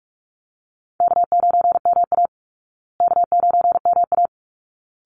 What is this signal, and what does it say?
Morse code, heard as a keyed tone, about 44 wpm, 700 Hz. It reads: K9MA K9MA
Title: Easy Call Signs Repeated 2x - 45wpm